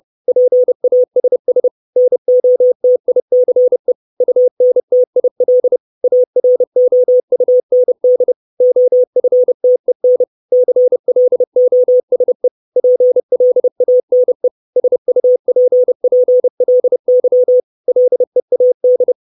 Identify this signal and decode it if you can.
Morse code, heard as a keyed tone, about 30 wpm, 500 Hz. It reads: PASS NOTICE UNTIL AROUND OFTEN CLOSE PLANE SUPPLY LEAD
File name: 632 Words 30WPM